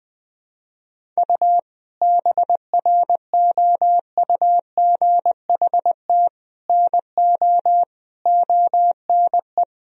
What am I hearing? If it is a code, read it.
Morse code, heard as a keyed tone, about 21 wpm, 700 Hz. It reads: U BROUGHT NO ONE
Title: Sentences from Top 500 Words and Abbreviations 20wpm